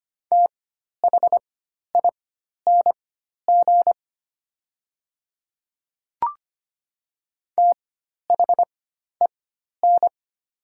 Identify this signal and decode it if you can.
Morse code, heard as a keyed tone, about 25 wpm, spaced as for 12 wpm, 700 Hz. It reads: THING THEN
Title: 4x Character Spacing - Top 100 Words - 25wpm